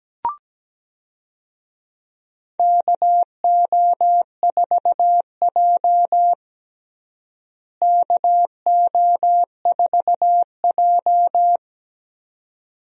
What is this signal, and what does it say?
Morse code, heard as a keyed tone, about 17 wpm, 700 Hz. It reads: KO4J KO4J
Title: USA Call Signs - 4 Characters - Repeated 2x - 17wpm